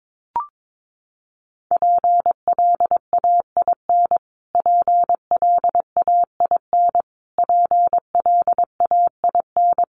Morse code, heard as a keyed tone, about 22 wpm, 700 Hz. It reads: PLAIN PLAIN PLAIN
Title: Top 500 Words Repeated 22wpm